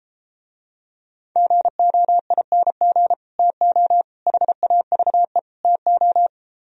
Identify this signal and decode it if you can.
Morse code, heard as a keyed tone, about 33 wpm, 700 Hz. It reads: GOING TO HAVE TO